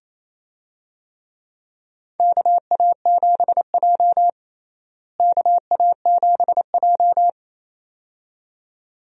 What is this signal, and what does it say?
Morse code, heard as a keyed tone, about 28 wpm, 700 Hz. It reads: KA7J KA7J
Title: USA Call Signs - 4 Characters - Repeated 2x - 28wpm